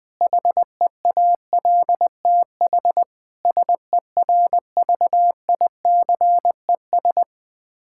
Morse code, heard as a keyed tone, about 20 wpm, 700 Hz. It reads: HEALTH SERVICES
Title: K1USN SST QSO Practice 20wpm